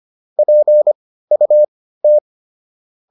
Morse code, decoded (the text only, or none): PUT